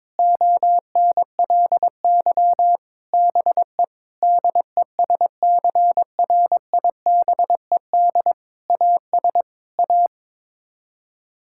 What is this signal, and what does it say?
Morse code, heard as a keyed tone, about 22 wpm, 700 Hz. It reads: ONLY BE DESCRIBED AS A
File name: practice / 22wpm / Sets of 8 Words - 22wpm